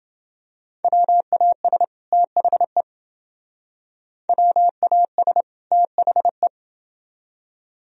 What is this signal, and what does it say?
Morse code, heard as a keyed tone, about 28 wpm, 700 Hz. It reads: WAS THE WAS THE